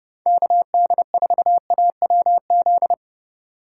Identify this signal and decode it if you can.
Morse code, heard as a keyed tone, about 30 wpm, 700 Hz. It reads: KD4AWZ